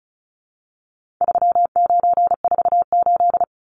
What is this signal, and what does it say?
Morse code, heard as a keyed tone, about 34 wpm, 700 Hz. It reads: 3948
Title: Numbers - 4-Digit 35wpm